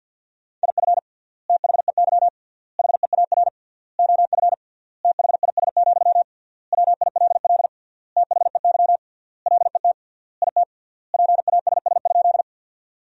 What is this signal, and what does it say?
Morse code, heard as a keyed tone, about 50 wpm, 700 Hz. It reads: IF THEY HEAR OF THIS, WILL THEY LET IT PASS?
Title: 3x Word-Spacing - Sentences from Top 400 Words - 50wpm